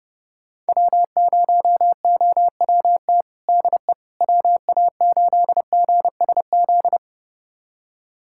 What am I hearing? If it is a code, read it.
Morse code, heard as a keyed tone, about 30 wpm, 700 Hz. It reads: W0OWT DE WA8GSZ
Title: USA Callsign DE Callsign 30wpm